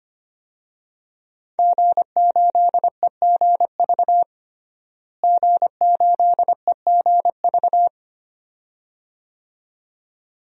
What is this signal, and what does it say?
Morse code, heard as a keyed tone, about 25 wpm, 700 Hz. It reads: G8EGV G8EGV
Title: DXCC-100 Call Signs - 5 Characters - Repeated 2x - 25wpm